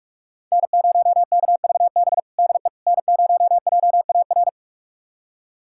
Morse code, heard as a keyed tone, about 46 wpm, 700 Hz. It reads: N0KVB DE N0JAR
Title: USA Callsign DE Callsign - 5 Characters - 45wpm